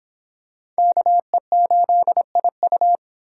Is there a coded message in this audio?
Yes: Morse code from 0.8 to 3.0 s, heard as a keyed tone, about 26 wpm, 700 Hz. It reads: KE8IU